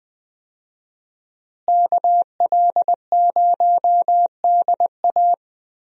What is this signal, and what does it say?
Morse code, heard as a keyed tone, about 20 wpm, 700 Hz. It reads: KL0DA
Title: USA Call Signs - 5 Characters - Repeated 2x - 20wpm